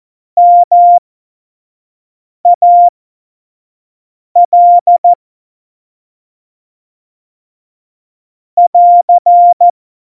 Morse code, heard as a keyed tone, about 14 wpm, spaced as for 5 wpm, 700 Hz. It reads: MAL <AR>
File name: Morsekurs / Witze / Witz 11 August 2020, 25 F70